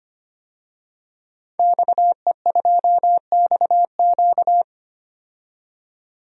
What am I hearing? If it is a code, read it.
Morse code, heard as a keyed tone, about 25 wpm, 700 Hz. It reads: XE2XQ
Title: DXCC-100 Call Signs - 5 Characters - 25wpm